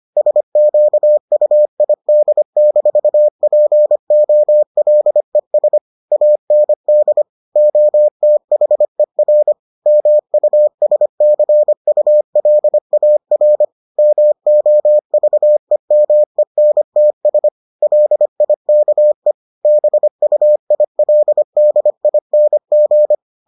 Morse code, heard as a keyed tone, about 25 wpm, 600 Hz. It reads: SQUID-POLES AND OTHER MUSCULAR MOVEMENTS LIKE BUILDING